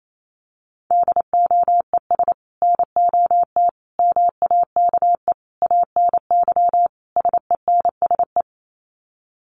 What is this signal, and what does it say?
Morse code, heard as a keyed tone, about 28 wpm, 700 Hz. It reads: DOES NOT MAKE ANY SENSE
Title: Sets of 5 Words Encore2 28wpm